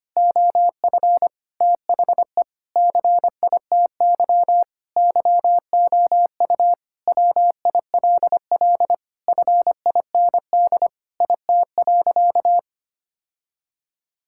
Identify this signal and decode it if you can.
Morse code, heard as a keyed tone, about 25 wpm, 700 Hz. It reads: OF THE CITY YOU WILL FIND IT.
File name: practice / 25wpm / Sentences with 4-Word Phrases - 25wpm